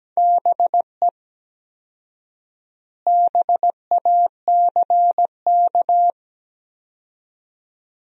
Morse code, heard as a keyed tone, about 17 wpm, 700 Hz. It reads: BE BACK
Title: 4x Word-Spacing - Sentences from Top 500 Words - 17wpm